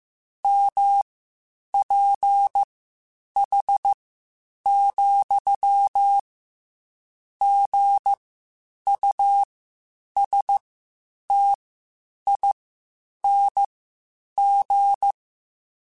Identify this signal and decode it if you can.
Morse code, heard as a keyed tone, about 15 wpm, spaced as for 8 wpm, 800 Hz. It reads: MPH, GUSTING
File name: Spaced Characters / Spaced Characters-mp3s / Weather 4 21 sp @13